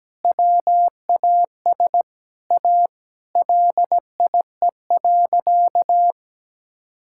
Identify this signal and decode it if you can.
Morse code, heard as a keyed tone, about 17 wpm, 700 Hz. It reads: WAS A LIE.
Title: Sentences from Top 1600 Words 17wpm